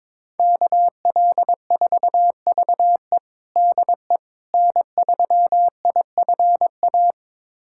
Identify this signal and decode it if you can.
Morse code, heard as a keyed tone, about 22 wpm, 700 Hz. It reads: KL4VE DE N3IFA
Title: USA Callsign DE Callsign - 5 Characters - 22wpm